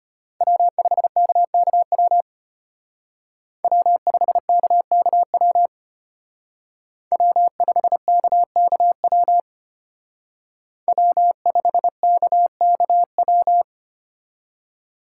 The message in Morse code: W5KKW W5KKW W5KKW W5KKW

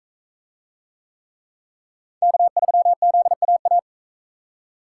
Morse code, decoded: K3ZAA

